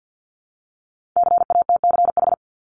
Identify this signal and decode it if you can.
Morse code, heard as a keyed tone, about 50 wpm, 700 Hz. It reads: CATCH
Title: Multi-letter Phonemes 50wpm